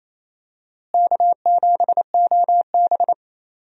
Morse code, heard as a keyed tone, about 28 wpm, 700 Hz. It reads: K7OB
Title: USA Call Signs - 4 Characters - Repeated 2x - 28wpm